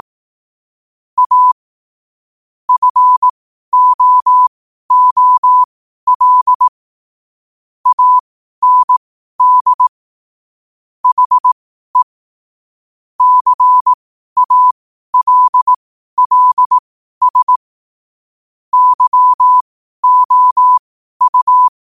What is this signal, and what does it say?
Morse code, heard as a keyed tone, about 18 wpm, spaced as for 12 wpm, 1000 Hz. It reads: A FOOL AND HE CALLS YOU
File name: qotdfiles / 12wpm